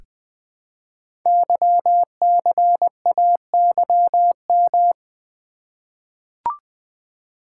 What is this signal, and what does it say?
Morse code, heard as a keyed tone, about 20 wpm, 700 Hz. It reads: YCAYM
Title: Code Groups - 5 Letters - 20wpm